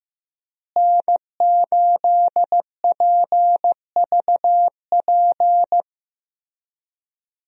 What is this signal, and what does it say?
Morse code, heard as a keyed tone, about 16 wpm, 700 Hz. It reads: N8PVP
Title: USA Call Signs - 5 Characters - Repeated 3x - 15wpm